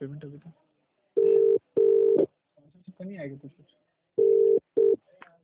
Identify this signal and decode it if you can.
Ringback tone (UK), 400 + 450 Hz, with speech.